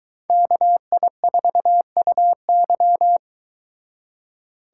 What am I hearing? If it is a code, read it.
Morse code, heard as a keyed tone, about 22 wpm, 700 Hz. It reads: KI4UY